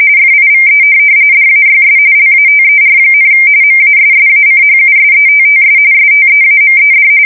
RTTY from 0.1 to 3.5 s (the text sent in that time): FIVE DAYS LATER,